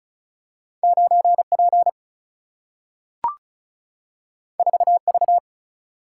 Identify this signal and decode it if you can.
Morse code, heard as a keyed tone, about 36 wpm, 700 Hz. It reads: 9 P 4 V
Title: Single-Letter Single-Number 35wpm